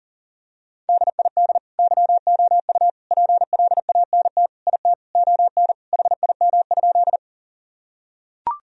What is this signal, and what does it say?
Morse code, heard as a keyed tone, about 40 wpm, 700 Hz. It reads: DID YOU PLANT IT ON HIM?
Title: Sentences from Top 1000 Words - Review Part 1 - 40wpm